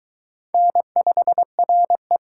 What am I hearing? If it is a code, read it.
Morse code, heard as a keyed tone, about 22 wpm, 700 Hz. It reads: N5RE